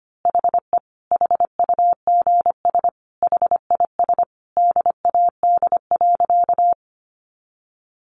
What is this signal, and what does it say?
Morse code, heard as a keyed tone, about 25 wpm, 700 Hz. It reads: HE HUGS HIS DAD.
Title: Easy Phrases 25wpm